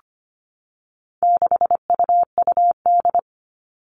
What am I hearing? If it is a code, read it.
Morse code, heard as a keyed tone, about 25 wpm, 700 Hz. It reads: 6UUD
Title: All Cut Numbers - 4-Digit 25wpm